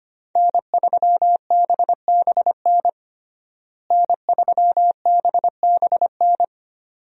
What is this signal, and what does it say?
Morse code, heard as a keyed tone, about 25 wpm, 700 Hz. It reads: N3BBN N3BBN
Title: USA Call Signs - 5 Characters - Repeated 2x - 25wpm